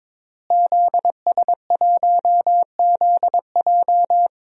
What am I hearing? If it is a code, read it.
Morse code, heard as a keyed tone, about 21 wpm, 700 Hz. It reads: ZS1ZJ
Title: DXCC-100 Call Signs - 5 Characters - Repeated 3x - 22wpm